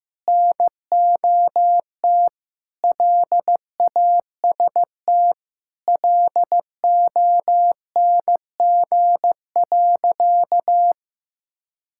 Morse code, heard as a keyed tone, about 15 wpm, 700 Hz. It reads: NOT LAST LONG.
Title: Sentences from Top 1600 Words 15wpm